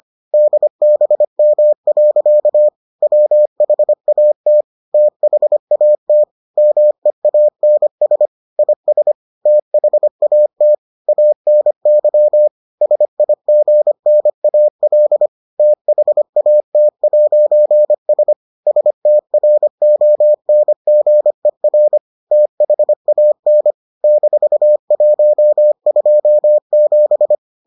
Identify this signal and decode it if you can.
Morse code, heard as a keyed tone, about 25 wpm, 600 Hz. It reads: DBM. WHAT THAT MEANS IS THAT ANY SIGNAL THAT'S STRONGER THAN -127